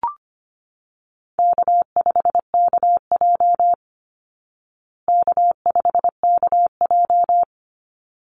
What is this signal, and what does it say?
Morse code, heard as a keyed tone, about 25 wpm, 700 Hz. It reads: K5KJ K5KJ
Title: Call Signs Repeated 3x - 25wpm